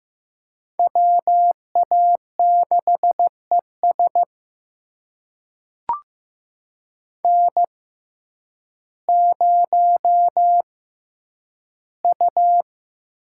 Morse code, heard as a keyed tone, about 15 wpm, spaced as for 9 wpm, 700 Hz. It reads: WA6ES N0U
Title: USA Call Signs - 5 Characters - 6x Character Spacing - 15wpm